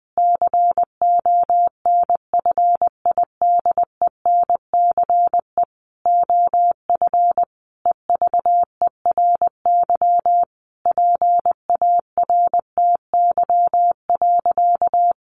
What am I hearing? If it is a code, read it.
Morse code, heard as a keyed tone, about 20 wpm, 700 Hz. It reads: CONFIDENCE OF EVERY PARTY.
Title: Sentences from Top 1500 Words 20wpm